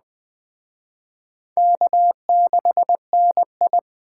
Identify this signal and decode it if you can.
Morse code, heard as a keyed tone, about 24 wpm, 700 Hz. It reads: K6NI
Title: Speed Racing - USA 1x2 Call Signs 20wpm